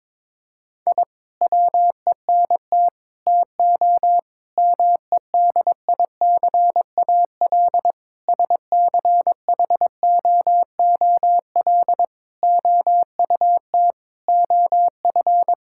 Morse code, heard as a keyed tone, about 22 wpm, 700 Hz. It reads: I WENT TO MEDICAL SCHOOL OUT OF
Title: Sentences from Top 1000 Words - Review Part 5 - 22wpm